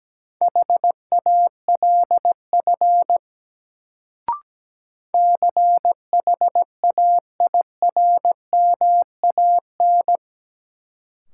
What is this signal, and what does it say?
Morse code, heard as a keyed tone, about 17 wpm, 700 Hz. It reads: HALF CHAIRMAN
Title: Sets of 3 Words - New Word Drill - 17wpm